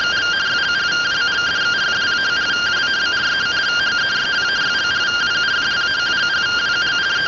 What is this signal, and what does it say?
Two-tone teleprinter signal, about 170 Hz shift.